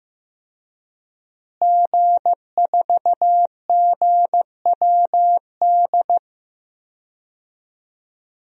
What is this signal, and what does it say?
Morse code, heard as a keyed tone, about 16 wpm, 700 Hz. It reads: G4GWD
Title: DXCC-100 Call Signs - 5 Characters - Repeated 2x - 15wpm